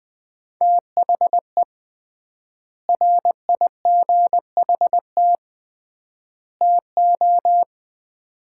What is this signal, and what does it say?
Morse code, heard as a keyed tone, about 20 wpm, 700 Hz. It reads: THE RIGHT TO
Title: 3x Word-Spacing - Sets of 3 Words - 20wpm